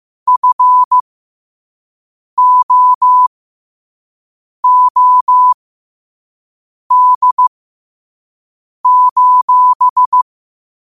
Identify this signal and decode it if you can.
Morse code, heard as a keyed tone, about 15 wpm, spaced as for 5 wpm, 1000 Hz. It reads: FOOD:
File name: 5wpm